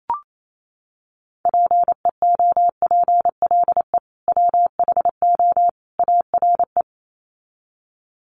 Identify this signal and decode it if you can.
Morse code, heard as a keyed tone, about 28 wpm, 700 Hz. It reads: PEOPLE WHO ARE
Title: Sets of 3 Words 28wpm